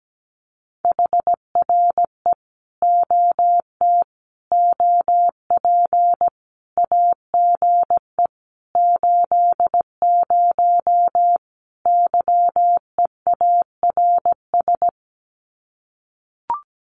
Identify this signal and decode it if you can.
Morse code, heard as a keyed tone, about 17 wpm, 700 Hz. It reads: HRE OT OP AGE 80 YEARS
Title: Common QSO Phrases 17wpm